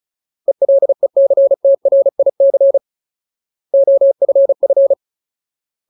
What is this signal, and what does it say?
Morse code, heard as a keyed tone, about 34 wpm, 550 Hz. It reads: ELECTRIC OFF
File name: Ultimate Word List / 35WPM Ultimate